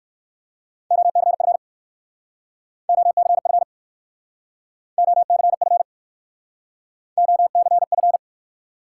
Morse code, heard as a keyed tone, about 53 wpm, 700 Hz. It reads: OCF OCF OCF OCF
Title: Speed-Racing Top 180 QSO Elements 45wpm